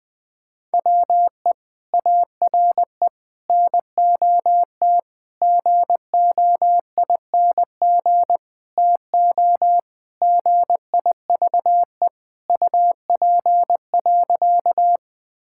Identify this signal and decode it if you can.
Morse code, heard as a keyed tone, about 20 wpm, 700 Hz. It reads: WE ARE NOT GOING TO GIVE UP.
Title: Sentences with 5-Word Phrases - 20wpm